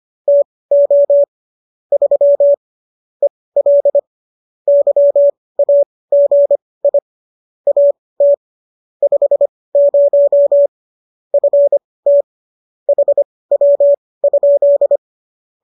Morse code, heard as a keyed tone, about 25 wpm, spaced as for 18 wpm, 570 Hz. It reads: TO 3 EL YAGI AT 50 FT HW?
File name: qso201 18